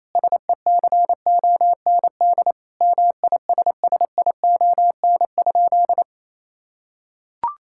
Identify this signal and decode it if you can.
Morse code, heard as a keyed tone, about 28 wpm, 700 Hz. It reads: SECOND MISSION?
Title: Sentences from Top 1600 Words 28wpm